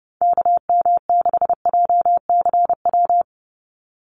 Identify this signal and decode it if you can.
Morse code, heard as a keyed tone, about 30 wpm, 700 Hz. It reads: KM6JCW
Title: USA Call Signs 30wpm